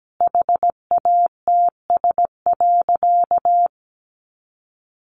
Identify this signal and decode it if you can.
Morse code, heard as a keyed tone, about 17 wpm, 700 Hz. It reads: HATS.
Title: Easy Phrases 17wpm